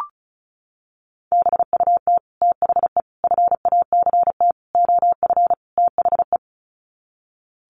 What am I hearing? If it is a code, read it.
Morse code, heard as a keyed tone, about 34 wpm, 700 Hz. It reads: BUT THE FACT OF THE
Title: Sets of 5 Words Encore2 35wpm